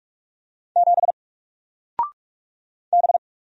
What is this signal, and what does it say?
Morse code, heard as a keyed tone, about 44 wpm, 700 Hz. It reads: 7 B
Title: Single Character (Letter:Number:Punctuation:Prosign) Alt 45wpm